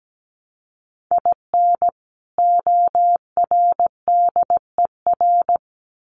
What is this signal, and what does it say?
Morse code, heard as a keyed tone, about 17 wpm, 700 Hz. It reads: IN ORDER
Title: Sets of 2 Words 17wpm